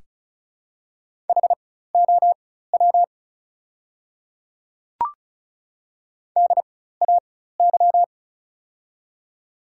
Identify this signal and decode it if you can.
Morse code, heard as a keyed tone, about 36 wpm, spaced as for 16 wpm, 690 Hz. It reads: HOW DAY
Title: 4x Character Spacing - Top 100 Words - 35wpm